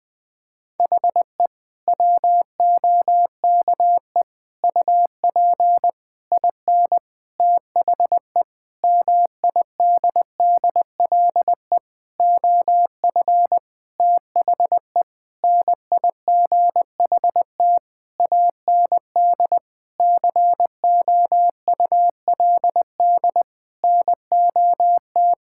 Morse code, heard as a keyed tone, about 19 wpm, 700 Hz. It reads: HE WOKE UP IN THE MIDDLE OF THE NIGHT AND COULD NOT
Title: Sentences with 7-Word Phrases - 20wpm